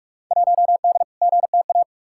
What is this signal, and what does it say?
Morse code, heard as a keyed tone, about 44 wpm, 700 Hz. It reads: 1D GTA
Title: Field Day Exchanges - 45wpm